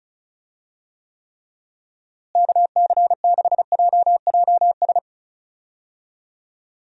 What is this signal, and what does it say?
Morse code, heard as a keyed tone, about 36 wpm, 700 Hz. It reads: KC6JJS
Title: USA Call Signs - 6 Characters - Repeated 2x - 35wpm